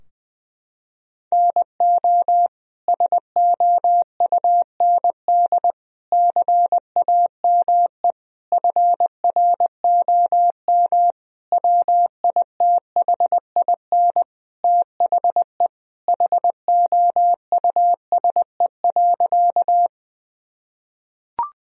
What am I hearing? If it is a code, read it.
Morse code, heard as a keyed tone, about 20 wpm, 700 Hz. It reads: NO SOUND CAME FROM WITHIN THE HOUSE.
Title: Sentences from Top 1000 Words - Review Part 4 - 20wpm